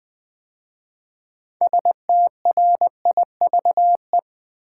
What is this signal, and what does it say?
Morse code, heard as a keyed tone, about 20 wpm, 700 Hz. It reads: STRIVE
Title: 6 Letter Words - First 3 Letters Given - 20wpm